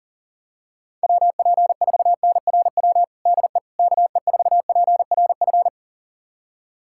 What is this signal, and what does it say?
Morse code, heard as a keyed tone, about 40 wpm, 700 Hz. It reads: WP4NRW DE KE4PRF
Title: USA Callsign DE Callsign - 6 Characters - 40wpm